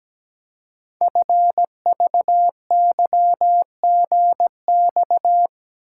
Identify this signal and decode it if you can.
Morse code, heard as a keyed tone, about 17 wpm, 700 Hz. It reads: FVYGX
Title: Code Groups - 5 Letters - 17wpm